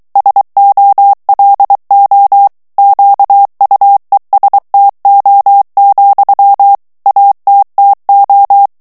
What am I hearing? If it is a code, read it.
Morse code, heard as a keyed tone, about 23 wpm, 800 Hz. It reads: SOLO QUESTO, ATTO
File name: ITALIEN texte clair 2 vitesse 22 mots minute